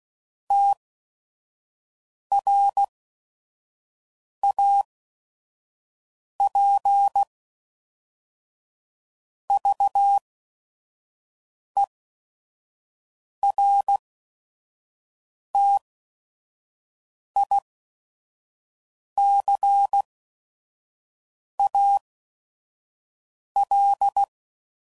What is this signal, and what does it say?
Morse code, heard as a keyed tone, about 16 wpm, spaced as for 5 wpm, 800 Hz. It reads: TRAP VERTICAL
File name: CW Code Groups @5WPM / fromSM's / Antennas